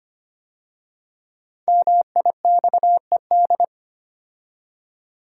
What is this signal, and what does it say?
Morse code, heard as a keyed tone, about 25 wpm, 700 Hz. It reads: MIXED
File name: 5 Letter Words - First 3 Letters Given - 25wpm